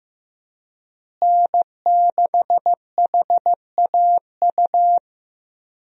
Morse code, heard as a keyed tone, about 15 wpm, 700 Hz. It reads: N6HAU
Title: USA Call Signs - 5 Characters - Repeated 2x - 15wpm